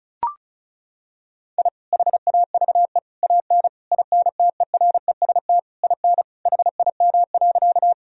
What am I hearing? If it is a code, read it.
Morse code, heard as a keyed tone, about 34 wpm, 700 Hz. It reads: I HAVE AN INTEREST IN HIM.
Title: Sentences from Top 1000 Words - Review Part 2 - 35wpm